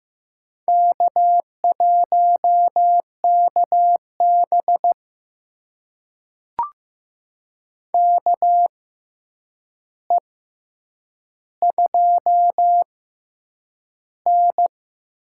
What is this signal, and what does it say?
Morse code, heard as a keyed tone, about 15 wpm, spaced as for 8 wpm, 700 Hz. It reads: K1KB KE2N
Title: USA Call Signs - 4 Characters - 6x Character Spacing - 15wpm